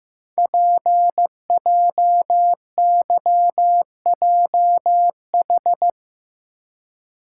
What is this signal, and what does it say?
Morse code, heard as a keyed tone, about 15 wpm, 700 Hz. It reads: PJYJH
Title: Code Groups - 5 Letters - 15wpm